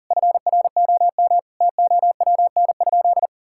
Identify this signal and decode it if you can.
Morse code, heard as a keyed tone, about 37 wpm, 700 Hz. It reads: FROM TOWN?
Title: Sentences from Top 1000 Words - Review Part 5 - 40wpm